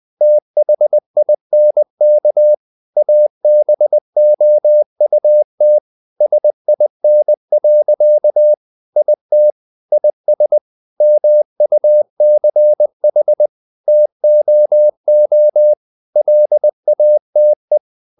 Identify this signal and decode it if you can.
Morse code, heard as a keyed tone, about 20 wpm, 600 Hz. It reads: THINK ABOUT SIN. IT IS MUCH TOO LATE
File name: the old man and the sea part 2 20wpm